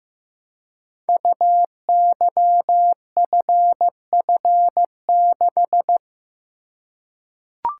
Morse code, heard as a keyed tone, about 15 wpm, 700 Hz. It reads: UYFF6
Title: Code Groups - 5 Letter:Numbers - 15wpm